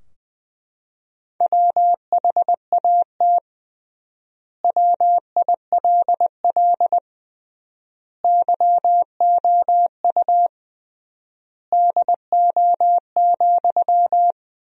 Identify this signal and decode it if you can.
Morse code, heard as a keyed tone, about 20 wpm, 700 Hz. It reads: WHAT WILL YOU DO,
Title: 3x Word-Spacing - Sentences from Top 100 Words - 20wpm